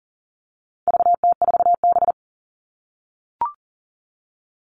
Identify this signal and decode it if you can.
Morse code, heard as a keyed tone, about 40 wpm, 700 Hz. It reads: VT4B
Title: All Cut Numbers - 4-Digit 40wpm